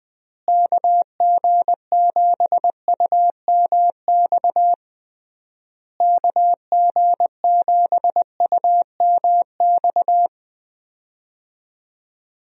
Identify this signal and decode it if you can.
Morse code, heard as a keyed tone, about 21 wpm, 700 Hz. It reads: KG7UMX KG7UMX
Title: USA Call Signs - 6 Characters - Repeated 3x - 20wpm